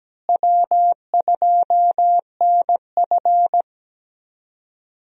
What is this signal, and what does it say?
Morse code, heard as a keyed tone, about 17 wpm, 700 Hz. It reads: W2NF
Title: USA Call Signs - 4 Characters - Repeated 3x - 17wpm